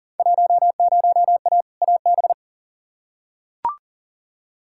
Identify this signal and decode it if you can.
Morse code, heard as a keyed tone, about 37 wpm, 700 Hz. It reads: 10A AB
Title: Field Day Exchanges Encore - 40wpm